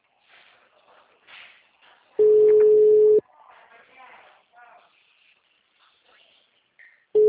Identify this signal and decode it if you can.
Ringback tone (Europe, 425 Hz), with speech.